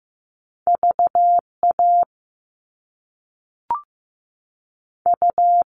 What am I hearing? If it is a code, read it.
Morse code, heard as a keyed tone, about 15 wpm, spaced as for 5 wpm, 700 Hz. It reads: VAU
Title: Top Two Letter Combinations in Words 15wpm